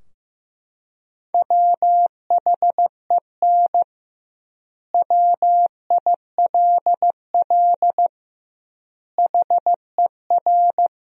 Morse code, heard as a keyed tone, about 15 wpm, 700 Hz. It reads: WHEN WILL HER
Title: 2x Word-Spacing - Sentences from Top 200 Words - 15wpm